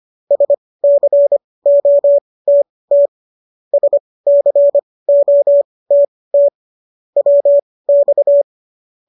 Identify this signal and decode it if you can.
Morse code, heard as a keyed tone, about 25 wpm, spaced as for 18 wpm, 570 Hz. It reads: SCOTT SCOTT WX
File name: qso223 18